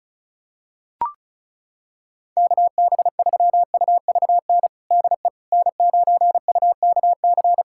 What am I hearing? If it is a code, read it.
Morse code, heard as a keyed tone, about 36 wpm, 690 Hz. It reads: KB3UVN DE N9UKC
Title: USA Callsign DE Callsign 35wpm